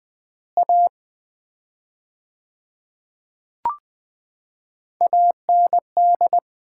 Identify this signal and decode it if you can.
Morse code, heard as a keyed tone, about 20 wpm, 700 Hz. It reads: A AND
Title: 4x Word-Spacing - Sets of 5 Words - 20wpm